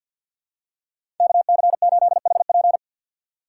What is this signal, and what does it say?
Morse code, heard as a keyed tone, about 51 wpm, 700 Hz. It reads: KC8HP
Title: USA Call Signs - 5 Characters - 2x Character Spacing - 50wpm